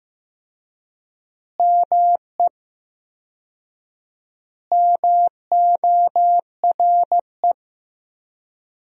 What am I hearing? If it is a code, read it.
Morse code, heard as a keyed tone, about 15 wpm, 700 Hz. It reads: ME MORE
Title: 4x Word-Spacing - Sentences from Top 200 Words - 15wpm